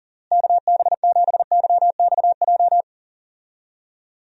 Morse code, heard as a keyed tone, about 40 wpm, 700 Hz. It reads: KB7YXJ
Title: USA Call Signs - 6 Characters - Repeated 2x - 40wpm